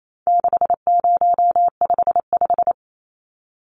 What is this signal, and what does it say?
Morse code, heard as a keyed tone, about 28 wpm, 700 Hz. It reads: 6055
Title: Numbers - 4-Digit 28wpm